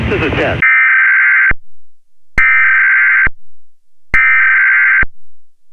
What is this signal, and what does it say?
Emergency Alert System (SAME) tones (header bursts), heard as harsh buzzing bursts, with speech. It is heard from 0.6 to 4.7 s.